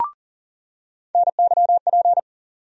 Morse code, heard as a keyed tone, about 40 wpm, 700 Hz. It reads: NYP
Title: USA Call Sign Postfixes - 3 Letters 40wpm